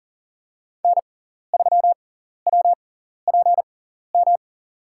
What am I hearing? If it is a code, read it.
Morse code, heard as a keyed tone, about 40 wpm, spaced as for 14 wpm, 700 Hz. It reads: N3WPM
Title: USA Call Signs - 5 Characters - 6x Character Spacing - 40wpm